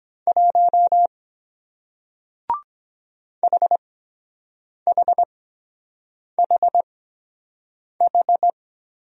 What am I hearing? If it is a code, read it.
Morse code, heard as a keyed tone, about 23 wpm, spaced as for 7 wpm, 700 Hz. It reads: 1 HHHH